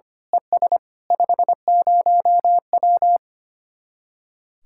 Morse code, heard as a keyed tone, about 24 wpm, 700 Hz. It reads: ES 50W
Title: Speed-Racing Common QSO Phrases - 25wpm